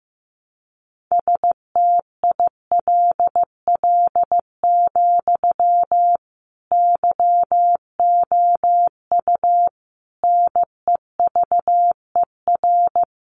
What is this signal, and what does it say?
Morse code, heard as a keyed tone, about 15 wpm, 700 Hz. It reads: STILL, YOU NEVER
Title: Sentences from Top 300 Words 15wpm